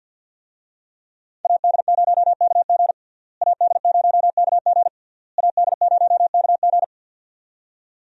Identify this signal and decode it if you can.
Morse code, heard as a keyed tone, about 52 wpm, 700 Hz. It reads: AD0KG AD0KG AD0KG
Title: USA Call Signs - 5 Characters - Repeated 3x - 50wpm